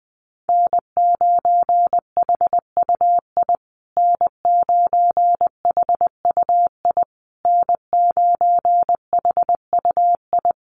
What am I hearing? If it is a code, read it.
Morse code, heard as a keyed tone, about 20 wpm, 700 Hz. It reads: N9HUI N9HUI N9HUI
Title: USA Call Signs Repeated 3x 20wpm